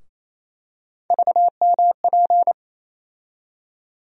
Morse code, heard as a keyed tone, about 28 wpm, 700 Hz. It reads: VMP
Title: USA Call Sign Postfixes - 3 Letters 28wpm